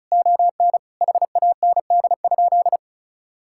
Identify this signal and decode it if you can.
Morse code, heard as a keyed tone, about 34 wpm, 700 Hz. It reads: ON HAND?
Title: Sentences from Top 1000 Words - Review Part 5 - 35wpm